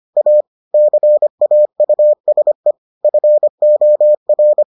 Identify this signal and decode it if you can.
Morse code, heard as a keyed tone, about 25 wpm, 600 Hz. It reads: A CAUSE FOR